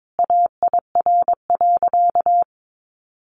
Morse code, heard as a keyed tone, about 22 wpm, 700 Hz. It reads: AIR.
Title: Sentences from Top 1200 Words 22wpm